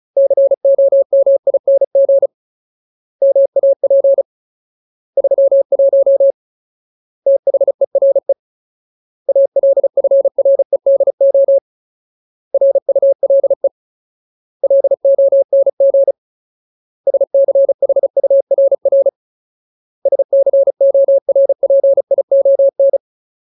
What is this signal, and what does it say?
Morse code, heard as a keyed tone, about 36 wpm, 550 Hz. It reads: COMING MAP 31 THERE ALFREDO RULE LONG SCHURR SCORPION
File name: Ultimate Word List / 35WPM Ultimate